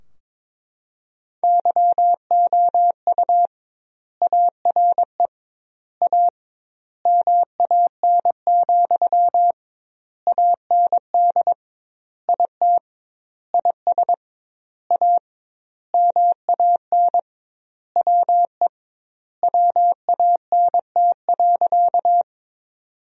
Morse code, heard as a keyed tone, about 22 wpm, 700 Hz. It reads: YOU ARE A MAN, AND IT IS A MAN WE WANT.
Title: 2x Word-Spacing - Sentences from Top 200 Words - 22wpm